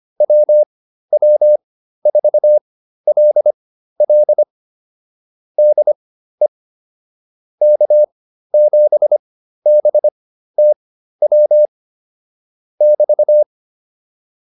Morse code, heard as a keyed tone, about 25 wpm, spaced as for 13 wpm, 600 Hz. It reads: WW4LL DE K7BTW <BT>